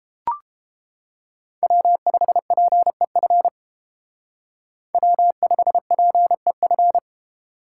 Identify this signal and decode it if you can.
Morse code, heard as a keyed tone, about 30 wpm, 700 Hz. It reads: W5PEF W5PEF